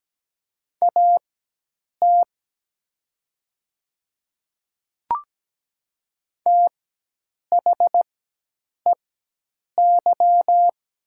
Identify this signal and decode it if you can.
Morse code, heard as a keyed tone, about 17 wpm, spaced as for 8 wpm, 700 Hz. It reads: AT THEY
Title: 4x Character Spacing - Top 100 Words - 17wpm